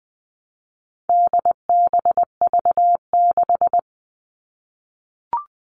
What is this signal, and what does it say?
Morse code, heard as a keyed tone, about 20 wpm, 700 Hz. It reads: DBV6
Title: All Cut Numbers - 4-Digit 20wpm